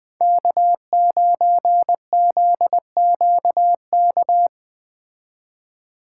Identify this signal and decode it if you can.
Morse code, heard as a keyed tone, about 20 wpm, 700 Hz. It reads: K9ZQK
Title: Speed Racing - USA 1x3 Call Signs 17wpm